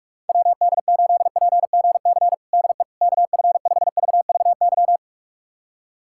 Morse code, heard as a keyed tone, about 46 wpm, 700 Hz. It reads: WD8PGC DE KF5VVY
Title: USA Callsign DE Callsign - 6 Characters - 45wpm